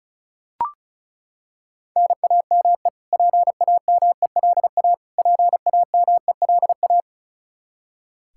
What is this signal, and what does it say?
Morse code, heard as a keyed tone, about 36 wpm, 700 Hz. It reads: NAME PAMELA PAMELA
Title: Practice QSO - Exchange 1 - Beginning - 35wpm